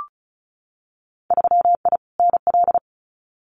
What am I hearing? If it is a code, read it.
Morse code, heard as a keyed tone, about 34 wpm, 700 Hz. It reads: 3I NL
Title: Winter Field Day Exchanges 35wpm